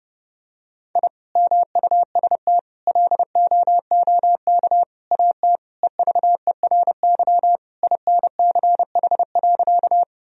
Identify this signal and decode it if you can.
Morse code, heard as a keyed tone, about 31 wpm, 700 Hz. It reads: I MUST LOOK AT EVERY INCH.
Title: Sentences from Top 1000 Words - Review Part 3 - 30wpm